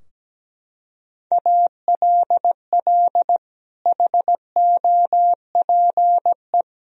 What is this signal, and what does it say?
Morse code, heard as a keyed tone, about 17 wpm, 700 Hz. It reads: ALL HOPE
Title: Sentences from Top 1000 Words - Review Part 4 - 17wpm